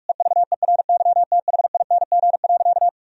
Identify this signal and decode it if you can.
Morse code, heard as a keyed tone, about 46 wpm, 700 Hz. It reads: EVERYTHING.
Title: Sentences from Top 1000 Words - Review Part 3 - 45wpm